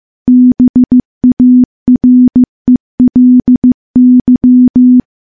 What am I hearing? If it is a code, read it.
Morse code, heard as a keyed tone, about 15 wpm, 250 Hz. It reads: BARELY
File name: morse / morse-training-15WPM